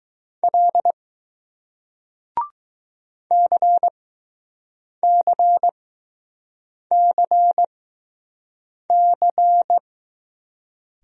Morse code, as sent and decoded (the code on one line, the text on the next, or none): .-.. / -.-. -.-. -.-. -.-.
L CCCC